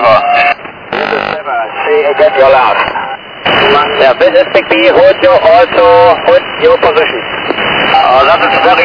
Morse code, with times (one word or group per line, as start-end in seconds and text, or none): none